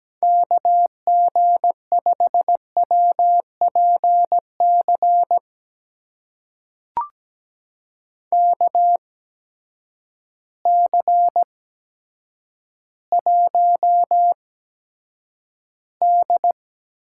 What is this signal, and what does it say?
Morse code, heard as a keyed tone, about 17 wpm, spaced as for 9 wpm, 700 Hz. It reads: KG5WPC KC1D
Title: USA Call Signs - 6 Characters - 8x Character Spacing - 17wpm